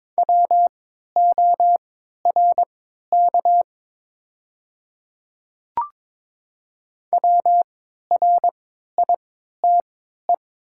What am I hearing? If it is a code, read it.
Morse code, heard as a keyed tone, about 23 wpm, spaced as for 13 wpm, 700 Hz. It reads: WORK WRITE
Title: 3x Character Spacing - Top 200 Words - 22wpm